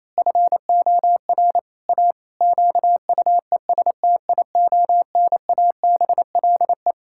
Morse code, heard as a keyed tone, about 27 wpm, 700 Hz. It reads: FOR A QUESTIONABLE
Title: Pangram Challenge v2 - 28wpm